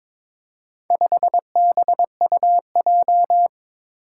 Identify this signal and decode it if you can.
Morse code, heard as a keyed tone, about 22 wpm, 700 Hz. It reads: 5BUJ
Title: Four Challenging Letter-Number Combinations 22wpm